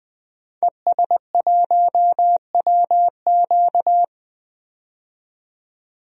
Morse code, heard as a keyed tone, about 20 wpm, 700 Hz. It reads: ES1WQ
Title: DXCC-100 Call Signs - 5 Characters - 20wpm